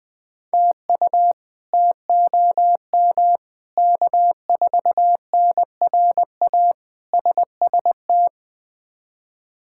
Morse code, heard as a keyed tone, about 20 wpm, 700 Hz. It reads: TU TOM K4NRA SST
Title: K1USN SST QSO Practice - 20wpm